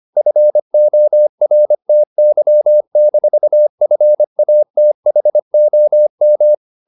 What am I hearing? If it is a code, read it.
Morse code, heard as a keyed tone, about 25 wpm, 600 Hz. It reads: FORTY-FATHOM
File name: the old man and the sea part 1 25wpm